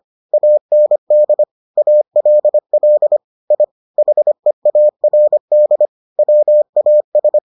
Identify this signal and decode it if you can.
Morse code, heard as a keyed tone, about 25 wpm, 600 Hz. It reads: AND ALL I HEARD WAS